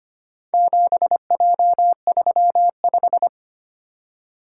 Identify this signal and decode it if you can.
Morse code, heard as a keyed tone, about 25 wpm, 700 Hz. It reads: 7J35
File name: practice / 25wpm / Four Letter-Number Combinations - 25wpm